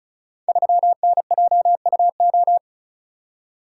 Morse code, heard as a keyed tone, about 36 wpm, 700 Hz. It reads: 3NJUO
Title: Code Groups - 5 Letter:Numbers - 35wpm